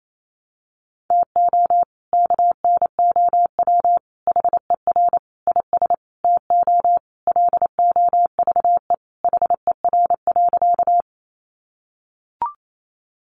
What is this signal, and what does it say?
Morse code, heard as a keyed tone, about 28 wpm, 700 Hz. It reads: TO KNOW HER IS TO LOVE HER.
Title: Sentences from Top 400 Words 28wpm